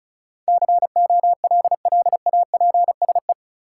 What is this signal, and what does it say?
Morse code, heard as a keyed tone, about 34 wpm, 700 Hz. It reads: COLLAPSE
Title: Sets of 5 Words Encore - New Word Drill - 35wpm